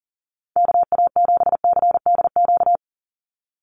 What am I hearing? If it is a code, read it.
Morse code, heard as a keyed tone, about 40 wpm, 700 Hz. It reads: KA7CDQ
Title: USA Call Signs 40wpm